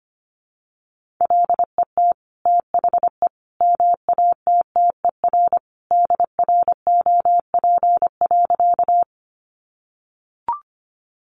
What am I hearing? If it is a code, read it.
Morse code, heard as a keyed tone, about 25 wpm, 700 Hz. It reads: LET THE MATTER DROP.
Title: Sentences from Top 1400 Words 25wpm